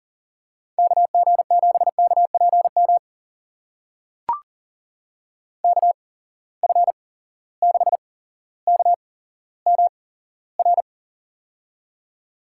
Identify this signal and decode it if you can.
Morse code, heard as a keyed tone, about 40 wpm, spaced as for 17 wpm, 700 Hz. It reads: KG7KPM KF6KMR
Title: USA Call Signs - 6 Characters - 8x Character Spacing - 40wpm